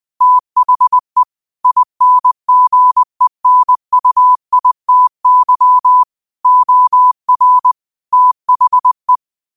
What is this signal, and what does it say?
Morse code, heard as a keyed tone, about 20 wpm, 1000 Hz. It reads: THE INGENUITY OR THE
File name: qotdfiles / 20wpm